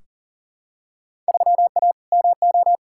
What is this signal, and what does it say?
Morse code, heard as a keyed tone, about 40 wpm, 700 Hz. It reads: 3A MO
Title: Field Day Exchanges - 40wpm